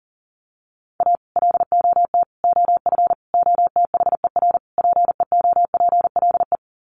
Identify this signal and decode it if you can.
Morse code, heard as a keyed tone, about 40 wpm, 700 Hz. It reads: A LOT OF OTHER PEOPLE
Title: Sets of 5 Words Encore2 40wpm